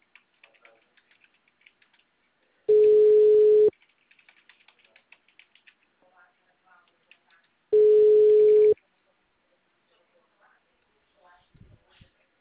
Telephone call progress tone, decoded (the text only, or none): ringback tone